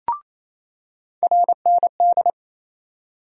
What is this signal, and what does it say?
Morse code, heard as a keyed tone, about 28 wpm, 700 Hz. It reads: RND
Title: USA Call Sign Postfixes - 3 Letters 28wpm